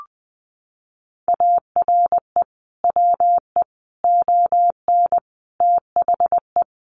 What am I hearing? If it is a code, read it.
Morse code, heard as a keyed tone, about 20 wpm, 700 Hz. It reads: ARE WE ON THE
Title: Sentences from Top 1400 Words 20wpm